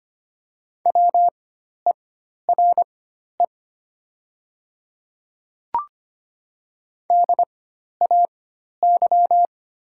Morse code, heard as a keyed tone, about 25 wpm, spaced as for 12 wpm, 700 Hz. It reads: WERE DAY
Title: 4x Character Spacing - Top 100 Words - 25wpm